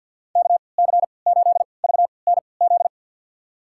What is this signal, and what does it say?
Morse code, heard as a keyed tone, about 51 wpm, spaced as for 28 wpm, 700 Hz. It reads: KC8VNZ
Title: USA Call Signs - 3x Character Spacing - 50wpm